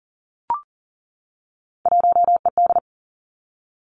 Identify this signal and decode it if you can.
Morse code, heard as a keyed tone, about 42 wpm, 700 Hz. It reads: 1ED
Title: Three Challenging Letter-Number Combinations 40wpm